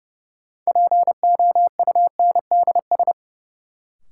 Morse code, heard as a keyed tone, about 30 wpm, 700 Hz. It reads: POUNDS
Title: Sentences from Top 1000 Words - Review Part 2 - New Word Drill - 30wpm